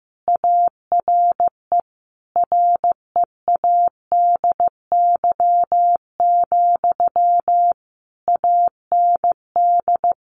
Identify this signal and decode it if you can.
Morse code, heard as a keyed tone, about 15 wpm, 700 Hz. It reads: ARE READY, AND
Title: Sentences from Top 500 Words 15wpm